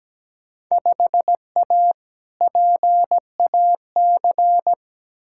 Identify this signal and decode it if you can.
Morse code, heard as a keyed tone, about 17 wpm, 700 Hz. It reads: 5A PAC
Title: Field Day Exchanges Encore - 17wpm